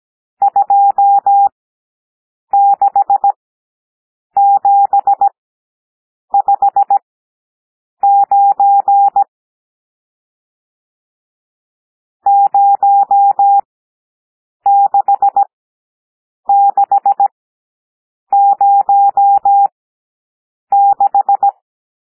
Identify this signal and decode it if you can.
Morse code, heard as a keyed tone, about 17 wpm, spaced as for 7 wpm, 800 Hz. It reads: 26759 06606
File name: SOUND NUMBERS 17 wpm